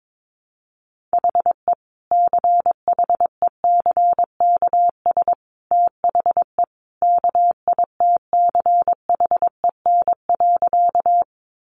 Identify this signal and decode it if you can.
Morse code, heard as a keyed tone, about 22 wpm, 700 Hz. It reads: HE CHECKS THE KITCHEN.
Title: Easy Phrases 22wpm